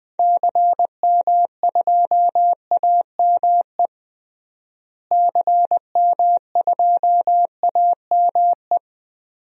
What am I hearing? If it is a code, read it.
Morse code, heard as a keyed tone, about 20 wpm, 700 Hz. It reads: CM2AME CM2AME
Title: DXCC-100 Call Signs - 6 Characters - Repeated 2x - 20wpm